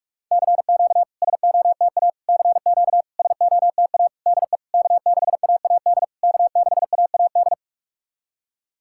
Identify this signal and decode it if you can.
Morse code, heard as a keyed tone, about 44 wpm, 690 Hz. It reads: CQ SOTA CQ SOTA DE K6AAD K6AAD
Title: SOTA QSO Practice 45wpm